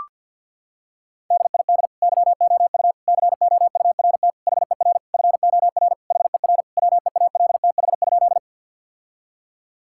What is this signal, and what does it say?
Morse code, heard as a keyed tone, about 50 wpm, 700 Hz. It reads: DID YOU COURT HER FOR HER WEALTH?
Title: Sentences from Top 1600 Words 50wpm